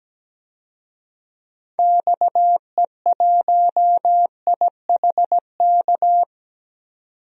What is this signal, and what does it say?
Morse code, heard as a keyed tone, about 17 wpm, 700 Hz. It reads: XE1IHK
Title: DXCC-100 Call Signs - 6 Characters - 17wpm